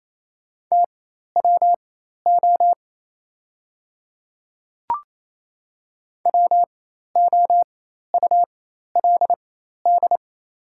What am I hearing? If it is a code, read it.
Morse code, heard as a keyed tone, about 28 wpm, spaced as for 13 wpm, 700 Hz. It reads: TWO WOULD
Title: 4x Character Spacing - Top 100 Words - 28wpm